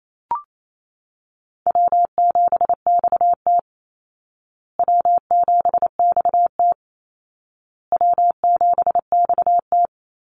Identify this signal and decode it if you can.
Morse code, heard as a keyed tone, about 28 wpm, 700 Hz. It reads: W7XT W7XT W7XT
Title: Call Signs Repeated 2x - 28wpm